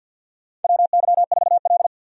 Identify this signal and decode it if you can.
Morse code, heard as a keyed tone, about 51 wpm, 700 Hz. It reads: WY4L
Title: USA Call Signs - 4 Characters - 50wpm